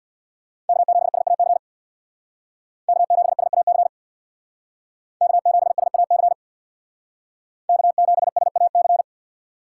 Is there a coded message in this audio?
Yes: Morse code from 0.7 to 9.0 s, heard as a keyed tone, about 63 wpm, 700 Hz. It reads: K7SAC K7SAC K7SAC K7SAC